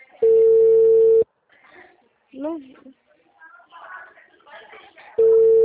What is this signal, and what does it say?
Ringback tone (China, 450 Hz), with speech.
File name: out- -2006